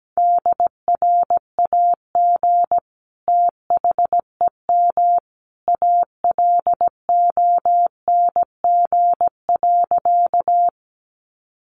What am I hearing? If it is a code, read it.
Morse code, heard as a keyed tone, about 17 wpm, 700 Hz. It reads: DRAG THEM ALONG.
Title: Sentences from Top 1500 Words 17wpm